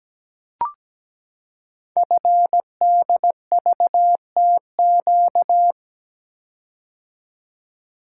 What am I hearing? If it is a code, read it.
Morse code, heard as a keyed tone, about 17 wpm, 700 Hz. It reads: FDVTQ
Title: Code Groups - 5 Letters - 17wpm